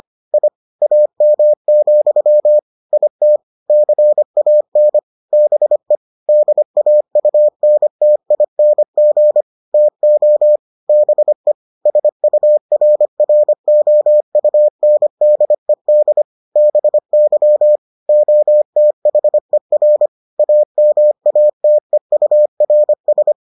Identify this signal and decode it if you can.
Morse code, heard as a keyed tone, about 25 wpm, 600 Hz. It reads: I AM, IT CAN BE DAUNTING TO BE SURROUNDED BY OTHER AMATEURS